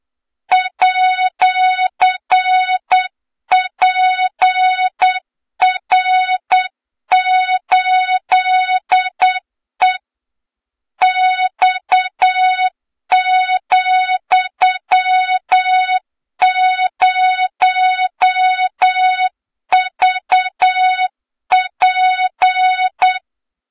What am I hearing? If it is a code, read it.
Morse code, heard as a keyed tone, about 8 wpm, 3700 Hz, with noise. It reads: @PR8E X,0VP
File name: Morze-HA2MN / hang / 23 GYAKORLAT